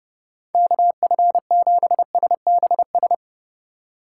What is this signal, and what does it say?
Morse code, heard as a keyed tone, about 31 wpm, 700 Hz. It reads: KF7SBS